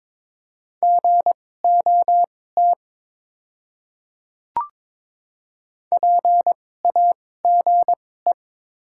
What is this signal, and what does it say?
Morse code, heard as a keyed tone, about 22 wpm, spaced as for 16 wpm, 700 Hz. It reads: GOT PAGE
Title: 2x Character Spacing - Top 300 Words - 22wpm